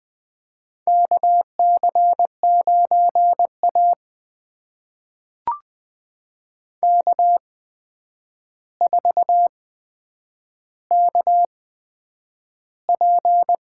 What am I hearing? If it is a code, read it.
Morse code, heard as a keyed tone, about 21 wpm, spaced as for 9 wpm, 700 Hz. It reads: KC9A K4KP
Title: USA Call Signs - 4 Characters - 8x Character Spacing - 20wpm